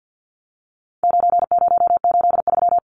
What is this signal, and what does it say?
Morse code, heard as a keyed tone, about 50 wpm, 700 Hz. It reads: 9083
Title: Numbers - 4-Digit 50wpm